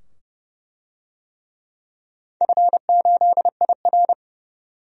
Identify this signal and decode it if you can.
Morse code, heard as a keyed tone, about 30 wpm, 700 Hz. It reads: F8IR
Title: DXCC-100 Call Signs - 4 Characters - 30wpm